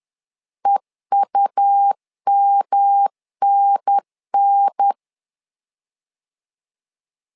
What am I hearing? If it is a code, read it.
Morse code, heard as a keyed tone, about 10 wpm, 800 Hz. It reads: EUMNN
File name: LEZIONI / 3 MWRO / 3 MW 10wpm 1959